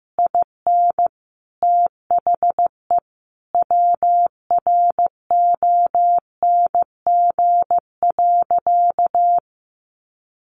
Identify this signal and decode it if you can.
Morse code, heard as a keyed tone, about 15 wpm, 700 Hz. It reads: IN THE WRONG.
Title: Sentences from Top 1200 Words 15wpm